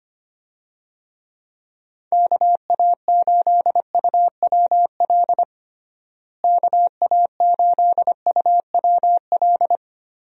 Morse code, heard as a keyed tone, about 25 wpm, 700 Hz. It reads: KA8UWL KA8UWL
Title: USA Call Signs - 6 Characters - Repeated 3x - 25wpm